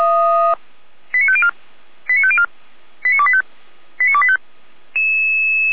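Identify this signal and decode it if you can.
Five-tone selective call (ZVEI), heard as quick runs of five tones, tones of about 70 ms. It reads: 89504 89504 89285 89285
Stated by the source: ZVEI five-tone selective call received by SDR.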